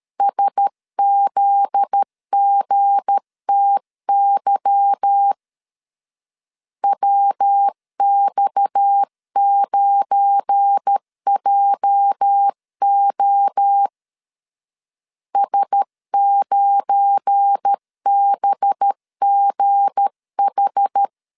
Morse code, heard as a keyed tone, about 13 wpm, 800 Hz. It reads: SZGTY WX9JO S9BGH
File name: LEZIONI / 9 28 12wpm 2351